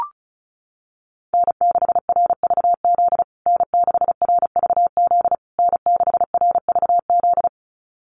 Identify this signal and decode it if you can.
Morse code, heard as a keyed tone, about 34 wpm, 700 Hz. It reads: N6RVZ N6RVZ N6RVZ
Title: USA Call Signs Repeated 3x 35wpm